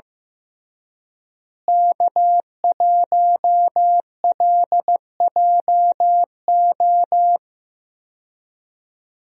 Morse code, heard as a keyed tone, about 15 wpm, 700 Hz. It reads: K1LJO
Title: USA Call Signs - 5 Characters - Repeated 2x - 15wpm